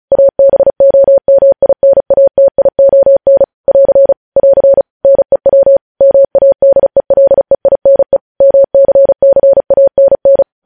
Morse code, heard as a keyed tone, about 36 wpm, 550 Hz. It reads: ABOMINATION <AR> <AR> NEW MADELEINE MCCANN
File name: podcast / News headlines-1x-35x35